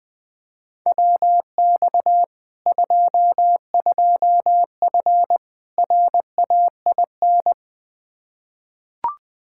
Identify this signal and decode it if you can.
Morse code, heard as a keyed tone, about 20 wpm, 700 Hz. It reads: WX 22F RAIN
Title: Practice QSO - Exchange 2 - Info - 20wpm